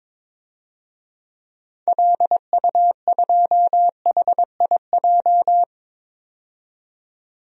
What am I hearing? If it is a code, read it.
Morse code, heard as a keyed tone, about 22 wpm, 700 Hz. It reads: LU2HIJ
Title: DXCC-100 Call Signs - 6 Characters - 22wpm